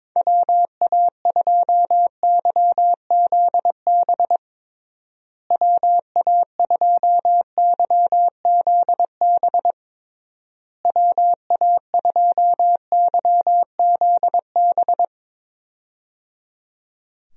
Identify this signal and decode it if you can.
Morse code, heard as a keyed tone, about 22 wpm, 700 Hz. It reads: WA2YZB WA2YZB WA2YZB
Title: USA Call Signs - 6 Characters - Repeated 3x - 22wpm